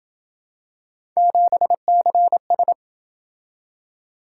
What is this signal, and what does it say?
Morse code, heard as a keyed tone, about 27 wpm, 700 Hz. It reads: 7CS